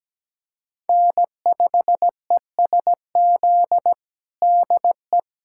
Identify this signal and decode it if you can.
Morse code, heard as a keyed tone, about 17 wpm, 700 Hz. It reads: N5ESZ DE
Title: USA Callsign DE Callsign - 5 Characters - 17wpm